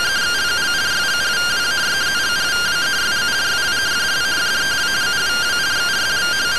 Two-tone teleprinter signal, about 170 Hz shift.